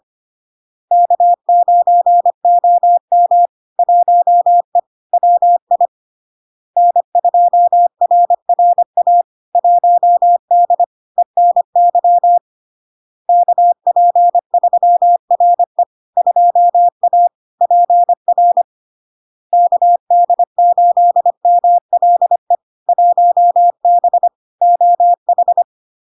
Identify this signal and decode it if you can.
Morse code, heard as a keyed tone, about 25 wpm, 700 Hz. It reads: K9OM 1E WI N2RRA 1D ENY KP3RE 2A PR KD8MLE 1B OH
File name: LARK-FD-2022-Log-in-CW-25wpm-700-Hz